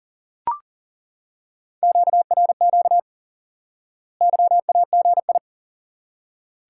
Morse code, heard as a keyed tone, about 37 wpm, 700 Hz. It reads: QRQ YAGI
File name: IQE Course 3 - Lesson 89 - YAGI - Two Elements - 40wpm